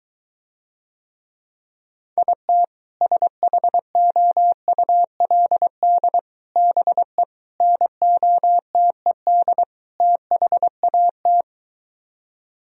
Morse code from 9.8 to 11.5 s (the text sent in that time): THAT